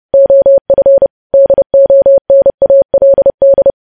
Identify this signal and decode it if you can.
Morse code, heard as a keyed tone, about 30 wpm, 550 Hz. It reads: OF DONALD
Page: news headlines in morse code at 30 wpm